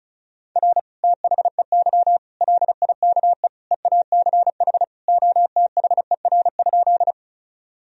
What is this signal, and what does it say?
Morse code, heard as a keyed tone, about 34 wpm, 690 Hz. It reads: R THEY LIKE EACH OTHER?
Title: Sentences from Top 200 Words and Abbreviations 35wpm